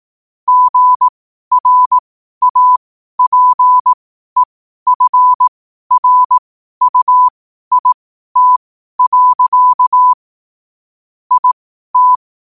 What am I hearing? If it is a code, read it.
Morse code, heard as a keyed tone, about 18 wpm, spaced as for 12 wpm, 1000 Hz. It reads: GRAPEFRUIT. IT
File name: qotdfiles / 12wpm